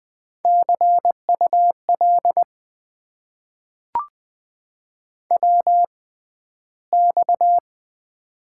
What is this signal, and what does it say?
Morse code, heard as a keyed tone, about 20 wpm, spaced as for 12 wpm, 700 Hz. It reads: CUL WX
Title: Top 90 QSO Elements - 6x Character Spacing - 20wpm